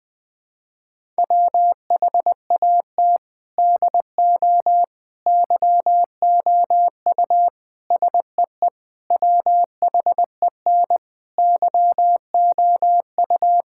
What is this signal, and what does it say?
Morse code, heard as a keyed tone, about 20 wpm, 700 Hz. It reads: WHAT DO YOU SEE WHEN YOU
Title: Sentences with 4-Word Phrases - 20wpm